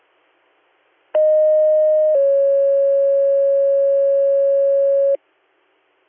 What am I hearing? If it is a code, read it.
Two-tone page, heard as a one-second tone, then a three-second tone, about 620 then 550 Hz. It is heard from 1.1 to 5.2 s.